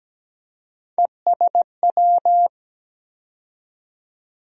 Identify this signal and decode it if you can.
Morse code, heard as a keyed tone, about 17 wpm, 700 Hz. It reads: ESW
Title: USA Call Sign Postfixes - 3 Letters 17wpm